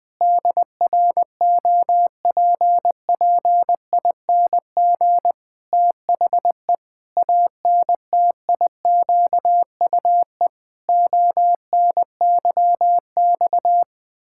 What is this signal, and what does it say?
Morse code, heard as a keyed tone, about 20 wpm, 700 Hz. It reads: DROPPING THE ANTIQUE ONYX
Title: Pangram Challenge v2 - 20wpm